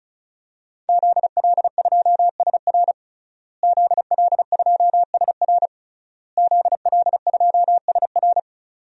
Morse code, heard as a keyed tone, about 34 wpm, 700 Hz. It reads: ZL2SR ZL2SR ZL2SR
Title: DXCC-100 Call Signs - 5 Characters - Repeated 3x - 35wpm